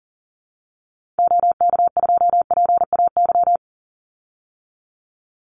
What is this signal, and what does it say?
Morse code, heard as a keyed tone, about 40 wpm, 700 Hz. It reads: OK2PAY
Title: Call Signs Repeated 3x - 40wpm